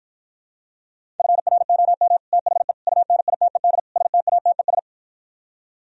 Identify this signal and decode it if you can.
Morse code, heard as a keyed tone, about 53 wpm, 700 Hz. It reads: FROM THE UNITED STATES